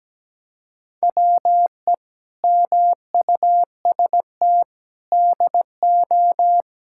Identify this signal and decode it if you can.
Morse code, heard as a keyed tone, about 17 wpm, 700 Hz. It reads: WE MUST DO
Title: Sentences from Top 1000 Words - Review Part 5 - 17wpm